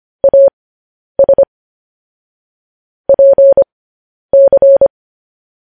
Morse code, heard as a keyed tone, about 25 wpm, spaced as for 10 wpm, 550 Hz. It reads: AS PC
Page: News Headlines in Morse Code at 10 WPM